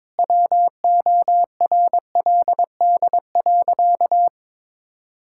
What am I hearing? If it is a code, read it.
Morse code, heard as a keyed tone, about 22 wpm, 700 Hz. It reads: WORLD.
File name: practice / 22wpm / Sentences with 8-Word Phrases - 22wpm